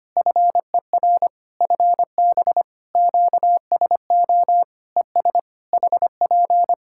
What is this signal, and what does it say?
Morse code, heard as a keyed tone, about 25 wpm, 700 Hz. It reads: FER FB QSO ES HP
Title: Practice QSO - Exchange 4 - Ending - 25wpm